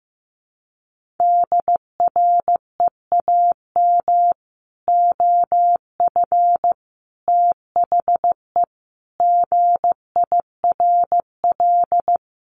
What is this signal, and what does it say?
Morse code, heard as a keyed tone, about 15 wpm, 700 Hz. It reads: DREAM OF THE GIRL
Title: Sentences from Top 1500 Words 15wpm